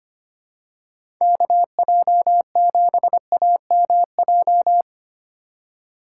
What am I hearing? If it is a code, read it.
Morse code, heard as a keyed tone, about 25 wpm, 700 Hz. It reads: KJ7AMJ
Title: USA Call Signs - 3x Character Spacing - 25wpm